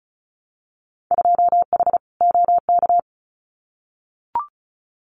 Morse code, heard as a keyed tone, about 34 wpm, 700 Hz. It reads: 2H OK
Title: Winter Field Day Exchanges Encore 35wpm